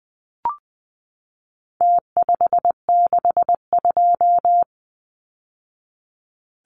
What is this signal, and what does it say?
Morse code, heard as a keyed tone, about 20 wpm, 700 Hz. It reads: T562
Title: Numbers - Contest Style - 4-Digit 20wpm